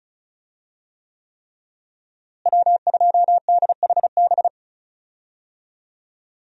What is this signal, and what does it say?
Morse code, heard as a keyed tone, about 36 wpm, 700 Hz. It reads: W2DHB
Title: USA Call Signs - 5 Characters - Repeated 3x - 35wpm